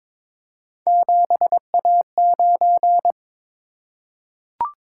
Morse code, heard as a keyed tone, about 22 wpm, 700 Hz. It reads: 7A9
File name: practice / 22wpm / Three Letter-Number Combinations - 22wpm